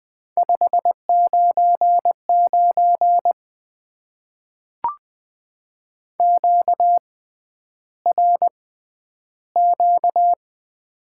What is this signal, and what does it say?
Morse code, heard as a keyed tone, about 20 wpm, spaced as for 10 wpm, 700 Hz. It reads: 599 QRQ
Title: Top 90 QSO Elements - 6x Character Spacing - 20wpm